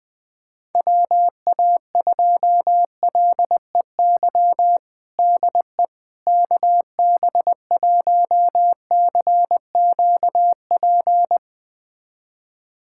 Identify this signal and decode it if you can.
Morse code, heard as a keyed tone, about 20 wpm, 700 Hz. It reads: WA2LEY DE KB1CQP
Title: USA Callsign DE Callsign 20wpm